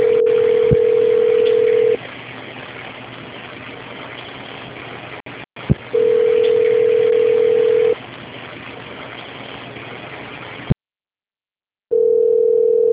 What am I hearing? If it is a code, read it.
Ringback tone (North America), 440 + 480 Hz.